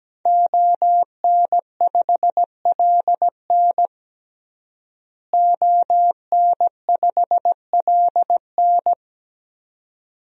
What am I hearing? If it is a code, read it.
Morse code, heard as a keyed tone, about 17 wpm, 700 Hz. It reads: ON5LN ON5LN
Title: DXCC-100 Call Signs - 5 Characters - Repeated 2x - 17wpm